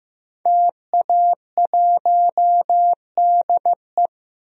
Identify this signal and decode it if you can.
Morse code, heard as a keyed tone, about 15 wpm, 700 Hz. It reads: TA1DE
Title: DXCC-100 Call Signs - 5 Characters - Repeated 2x - 15wpm